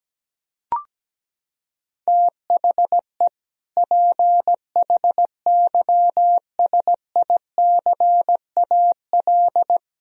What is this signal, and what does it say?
Morse code, heard as a keyed tone, about 17 wpm, 700 Hz. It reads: THE PHYSICAL
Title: Sentences from Top 1000 Words - Review Part 4 - 17wpm